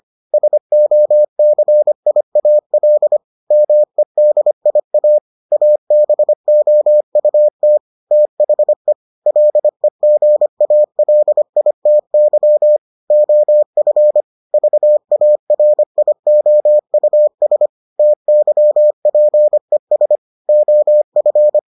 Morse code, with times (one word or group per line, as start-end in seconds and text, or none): 0.3-3.2 SOCIAL
3.5-5.2 MEDIA
5.5-7.8 ABOUT
8.1-8.9 THE
9.3-12.8 LEGALITY
13.1-14.2 OF
14.5-17.7 VARIOUS
18.0-20.2 TYPES
20.5-21.6 OF